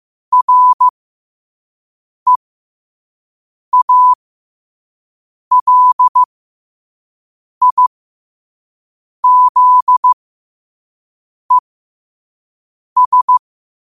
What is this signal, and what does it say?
Morse code, heard as a keyed tone, about 15 wpm, spaced as for 5 wpm, 1000 Hz. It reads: REALIZES
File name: qotdfiles / 5wpm